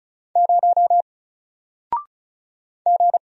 Morse code, heard as a keyed tone, about 36 wpm, 700 Hz. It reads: 0 G
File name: Single-Letter-Number-35wpm